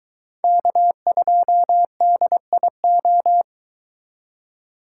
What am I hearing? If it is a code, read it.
Morse code, heard as a keyed tone, about 22 wpm, 700 Hz. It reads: K2DIO